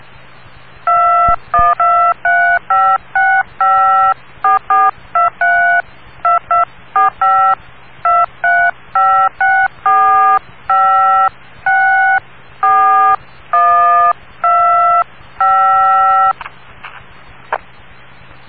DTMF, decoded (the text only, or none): 212356544232245 23564564125